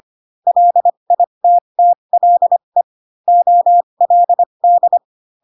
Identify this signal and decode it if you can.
Morse code, heard as a keyed tone, about 25 wpm, spaced as for 22 wpm, 700 Hz. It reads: LITTLE OLD
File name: Pinocchio / Pinocchio ch3 18 25 ik1hge